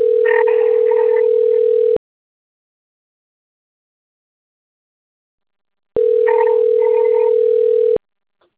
Ringback tone (North America), 440 + 480 Hz.
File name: out- - - - - .4090